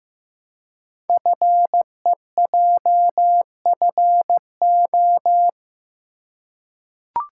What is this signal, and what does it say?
Morse code, heard as a keyed tone, about 15 wpm, 700 Hz. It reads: FEJFO
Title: Code Groups - 5 Letters - 15wpm